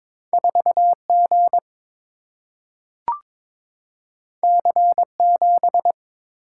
Morse code, heard as a keyed tone, about 22 wpm, 700 Hz. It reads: 4 G C 7
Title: Single-Letter Single-Number 22wpm